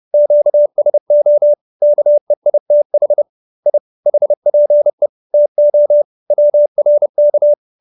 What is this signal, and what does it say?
Morse code, heard as a keyed tone, about 30 wpm, 590 Hz. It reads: QSO KEITH I HPE TO WRK